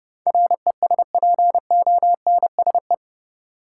Morse code, heard as a keyed tone, about 30 wpm, 700 Hz. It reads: RESPONSE
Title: Sentences from Top 1400 Words - New Word Drill - 30wpm